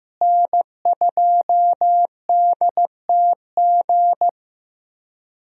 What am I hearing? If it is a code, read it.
Morse code, heard as a keyed tone, about 16 wpm, 700 Hz. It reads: N2DTG
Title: USA Call Signs - 5 Characters - Repeated 3x - 15wpm